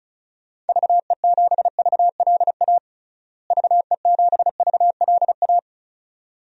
Morse code, heard as a keyed tone, about 34 wpm, 700 Hz. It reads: VE7VLA VE7VLA
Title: DXCC-100 Call Signs - 6 Characters - Repeated 2x - 35wpm